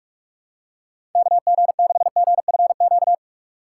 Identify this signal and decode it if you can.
Morse code, heard as a keyed tone, about 46 wpm, 700 Hz. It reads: KG6GFQ
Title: USA Call Signs - 6 Characters - 2x Character Spacing - 45wpm